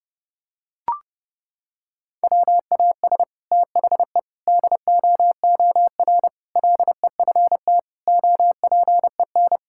Morse code, heard as a keyed tone, about 30 wpm, 700 Hz. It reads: WAS THE DOOR LEFT OPEN
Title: Sentences from Top 1000 Words - Review Part 6 - 30wpm